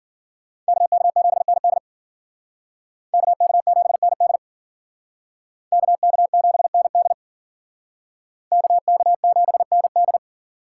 Morse code, heard as a keyed tone, about 51 wpm, 700 Hz. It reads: KK7ND KK7ND KK7ND KK7ND